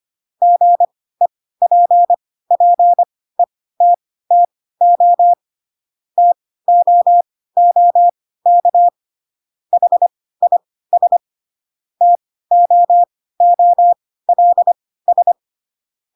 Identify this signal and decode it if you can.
Morse code, heard as a keyed tone, about 25 wpm, spaced as for 16 wpm, 700 Hz. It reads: GEPPETTO TOOK HIS TOOLS
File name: Pinocchio / Pinocchio ch3 10 25 ik1hge